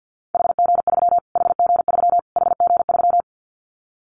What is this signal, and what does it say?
Morse code, heard as a keyed tone, about 50 wpm, 700 Hz. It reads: HG3 HG3 HG3
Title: Callsign Prefixes for Top 25 DXCC - Repeated 3x - 50wpm